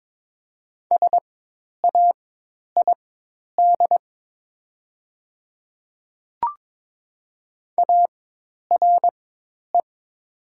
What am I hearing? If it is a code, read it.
Morse code, heard as a keyed tone, about 22 wpm, spaced as for 10 wpm, 700 Hz. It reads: SAID ARE
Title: 4x Character Spacing - Top 200 Words - 22wpm